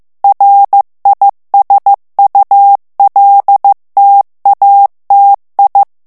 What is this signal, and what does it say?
Morse code, heard as a keyed tone, about 15 wpm, 800 Hz. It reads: RISULTATI
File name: morse mp3 / ITALIEN texte clair 2 vitesse 14 mots minute